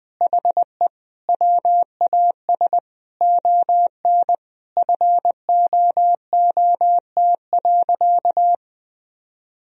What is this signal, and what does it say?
Morse code, heard as a keyed tone, about 20 wpm, 700 Hz. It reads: HE WAS ON FOOT.
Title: Sentences from Top 1000 Words - Review Part 3 - 20wpm